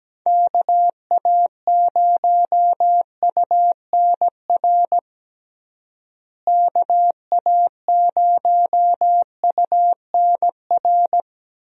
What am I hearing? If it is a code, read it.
Morse code, heard as a keyed tone, about 17 wpm, 700 Hz. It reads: KA0UNR KA0UNR
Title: USA Call Signs - 6 Characters - Repeated 2x - 17wpm